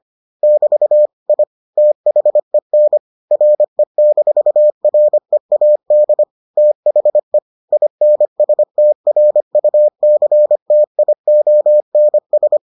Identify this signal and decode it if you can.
Morse code, heard as a keyed tone, about 25 wpm, 600 Hz. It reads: <BT> I THEN RE-READ THE INSTRUCTIONS